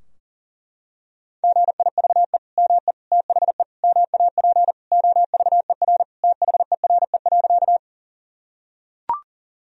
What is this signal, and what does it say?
Morse code, heard as a keyed tone, about 40 wpm, 700 Hz. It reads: GIVE ME THE MAP OVER THERE.
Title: Sentences from Top 1000 Words - Review Part 2 - 40wpm